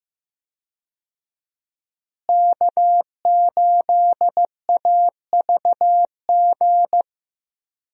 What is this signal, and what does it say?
Morse code, heard as a keyed tone, about 15 wpm, 700 Hz. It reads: K8AVG
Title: USA Call Signs - 5 Characters - Repeated 3x - 15wpm